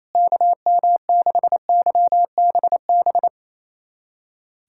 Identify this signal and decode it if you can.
Morse code, heard as a keyed tone, about 28 wpm, 700 Hz. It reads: KM6YBB
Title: USA Callsign DE Callsign - 6 Characters - 28wpm